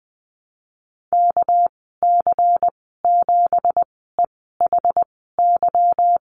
Morse code, heard as a keyed tone, about 20 wpm, spaced as for 14 wpm, 700 Hz. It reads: KC7EHY
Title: 245 Lesson 39g - Letter z - Call Signs 20wpm